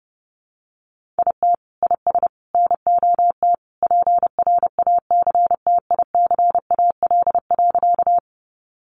Morse code, heard as a keyed tone, about 30 wpm, 700 Hz. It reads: IT IS NOT PRACTICAL.
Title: Sentences from Top 1400 Words 30wpm